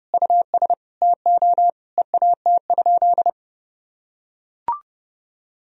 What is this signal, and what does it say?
Morse code, heard as a keyed tone, about 30 wpm, 700 Hz. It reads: US TO EAT?
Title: Sentences from Top 1000 Words - Review Part 3 - 30wpm